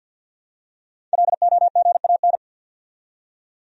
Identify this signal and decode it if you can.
Morse code, heard as a keyed tone, about 53 wpm, 700 Hz. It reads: LOGAN
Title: USA Common Names - 50wpm